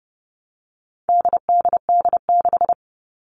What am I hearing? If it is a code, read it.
Morse code, heard as a keyed tone, about 30 wpm, 700 Hz. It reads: DDD6
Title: All Cut Numbers - 4-Digit 30wpm